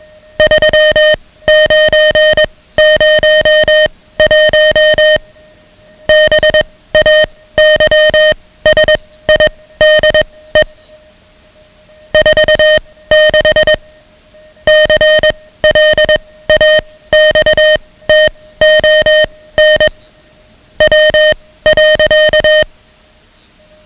Morse code, heard as a keyed tone, about 21 wpm, spaced as for 16 wpm, 620 Hz, with noise. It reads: 3901 BAYSIDE 46 CLAXTON W.
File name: ham radio / CodeQuick / Code Quick 10